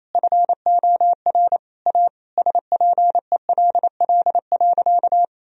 Morse code, heard as a keyed tone, about 28 wpm, 700 Hz. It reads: FOR A SPELL.
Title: Sentences from Top 1000 Words - Review Part 1 - 28wpm